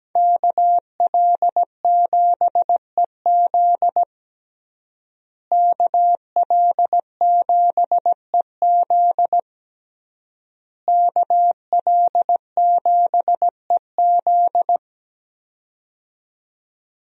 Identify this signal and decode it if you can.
Morse code, heard as a keyed tone, about 17 wpm, 700 Hz. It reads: KL7EZ KL7EZ KL7EZ
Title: DXCC-100 Call Signs - 5 Characters - Repeated 3x - 17wpm